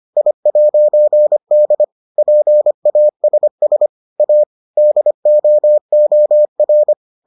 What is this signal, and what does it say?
Morse code, heard as a keyed tone, about 25 wpm, 600 Hz. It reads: I'D PASS A DOOR